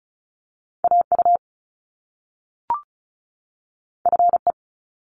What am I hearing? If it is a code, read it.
Morse code, heard as a keyed tone, about 36 wpm, 700 Hz. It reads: AU FE
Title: Top Two Letter Combinations in Words 35wpm